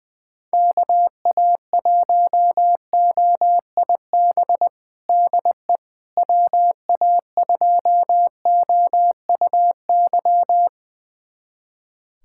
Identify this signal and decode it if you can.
Morse code, heard as a keyed tone, about 21 wpm, 700 Hz. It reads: KA1OIB DE WA2OUY
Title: USA Callsign DE Callsign 20wpm